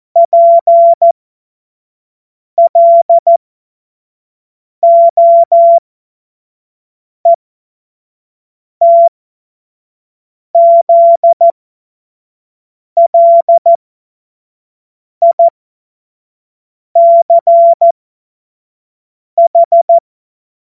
Morse code, heard as a keyed tone, about 14 wpm, spaced as for 5 wpm, 680 Hz. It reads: PLOETZLICH
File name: Morsekurs / Witze / Witz 02 April 2024 25 F70